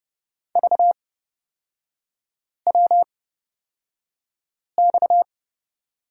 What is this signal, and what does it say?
Morse code, heard as a keyed tone, about 30 wpm, spaced as for 5 wpm, 690 Hz. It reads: VWX
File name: Lesson 37e - Letter x - Three Characters 30wpm